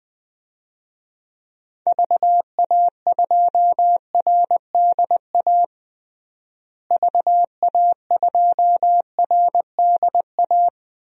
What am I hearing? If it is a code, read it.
Morse code, heard as a keyed tone, about 21 wpm, 700 Hz. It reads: VA2RDA VA2RDA
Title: DXCC-100 Call Signs - 6 Characters - Repeated 3x - 20wpm